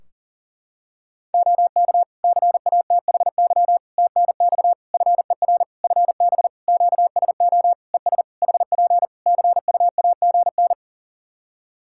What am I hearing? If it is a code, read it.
Morse code, heard as a keyed tone, about 37 wpm, 700 Hz. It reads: OK CATHY TNX FER FB QSO ES HP CUAGN
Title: Practice QSO - Exchange 4 - Ending - 40wpm